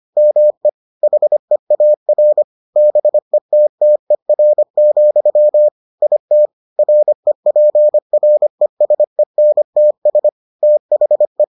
Morse code, heard as a keyed tone, about 25 wpm, 600 Hz. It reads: ME HEAR BETTER, IT REPRESENTS THE